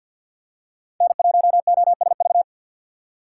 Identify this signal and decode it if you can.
Morse code, heard as a keyed tone, about 52 wpm, 700 Hz. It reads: N1OSV
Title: USA Call Signs - 5 Characters - 4x Character Spacing - 50wpm